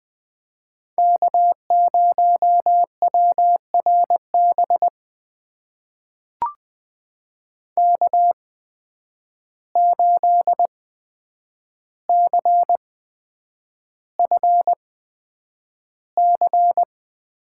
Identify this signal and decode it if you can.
Morse code, heard as a keyed tone, about 20 wpm, spaced as for 9 wpm, 700 Hz. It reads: K0WRB K8CFC
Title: USA Call Signs - 5 Characters - 8x Character Spacing - 20wpm